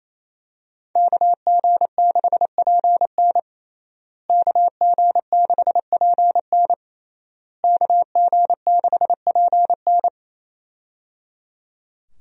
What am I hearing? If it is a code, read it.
Morse code, heard as a keyed tone, about 28 wpm, 700 Hz. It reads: KG6PN KG6PN KG6PN
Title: USA Call Signs - 5 Characters - Repeated 3x - 28wpm